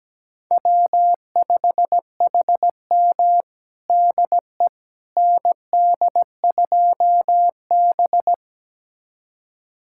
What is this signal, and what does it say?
Morse code, heard as a keyed tone, about 17 wpm, 700 Hz. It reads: W5HM DE ND2B
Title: USA Callsign DE Callsign - 4 Characters - 17wpm